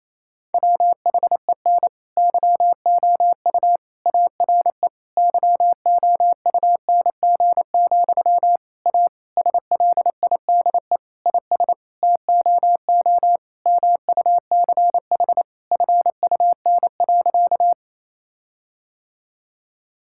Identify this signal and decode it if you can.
Morse code, heard as a keyed tone, about 27 wpm, 700 Hz. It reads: WHEN YOU ARE YOUNG, A SLIDE IS TOO MUCH FUN.
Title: Sentences from Top 1600 Words 28wpm